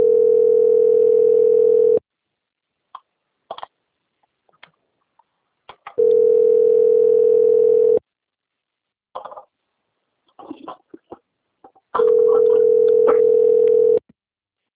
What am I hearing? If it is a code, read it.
Ringback tone (North America), 440 + 480 Hz.